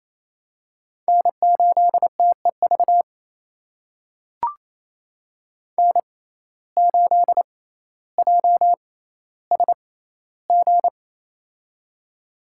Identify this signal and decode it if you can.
Morse code, heard as a keyed tone, about 28 wpm, spaced as for 14 wpm, 700 Hz. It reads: N8TEV N8JSG
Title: USA Call Signs - 5 Characters - 6x Character Spacing - 28wpm